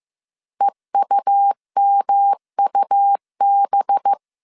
Morse code, heard as a keyed tone, about 15 wpm, 800 Hz. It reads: EUMUB